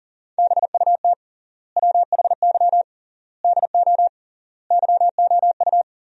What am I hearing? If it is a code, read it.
Morse code, heard as a keyed tone, about 37 wpm, 700 Hz. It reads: BUT WHY DO YOU
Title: 3x Word-Spacing - Sentences from Top 200 Words - 40wpm